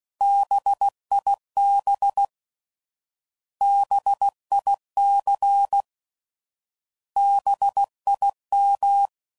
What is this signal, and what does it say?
Morse code, heard as a keyed tone, about 16 wpm, 800 Hz. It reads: BIB BIC BIM